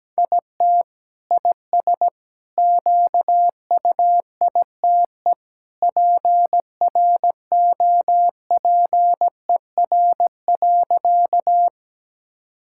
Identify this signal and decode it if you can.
Morse code, heard as a keyed tone, about 17 wpm, 700 Hz. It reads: IT IS QUITE PROPER.
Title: Sentences from Top 1000 Words - Review Part 5 - 17wpm